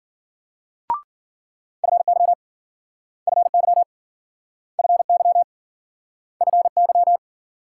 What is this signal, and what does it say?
Morse code, heard as a keyed tone, about 53 wpm, 700 Hz. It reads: FY FY FY FY